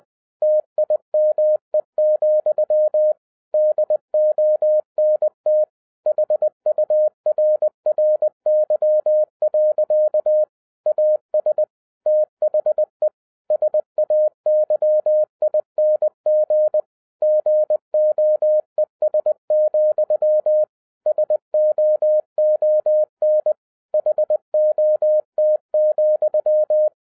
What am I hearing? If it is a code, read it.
Morse code, heard as a keyed tone, about 20 wpm, 610 Hz. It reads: TIME, DONT HURRY. AS THE SAYING GOES, SOON HOT,
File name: 2011-10-mp3 / n13-TIPS FROM MY 34 YEARS OF EXPERIENCE 20wpm